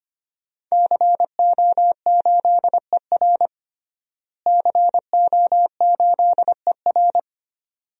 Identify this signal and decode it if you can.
Morse code, heard as a keyed tone, about 25 wpm, 700 Hz. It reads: CO8ER CO8ER
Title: DXCC-100 Call Signs - 5 Characters - Repeated 2x - 25wpm